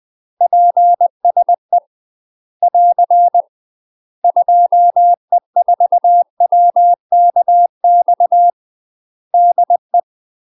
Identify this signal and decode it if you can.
Morse code, heard as a keyed tone, about 20 wpm, 700 Hz. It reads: PSE <AR> 2E4WKX DE
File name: UK MASTER QSOs TRAINING / 20WPM / UK MASTER QSOs 0018